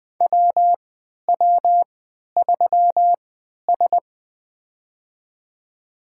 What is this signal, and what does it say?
Morse code, heard as a keyed tone, about 20 wpm, spaced as for 11 wpm, 700 Hz. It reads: WW3S
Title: USA Call Signs - 4 Characters - 3x Character Spacing - 20wpm